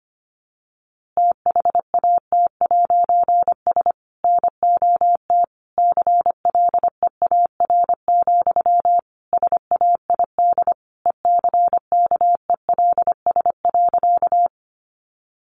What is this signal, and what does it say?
Morse code, heard as a keyed tone, about 25 wpm, 700 Hz. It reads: THAT'S NOT CLEAR, SAID ECKELS.
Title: A Sound Of Thunder 25wpm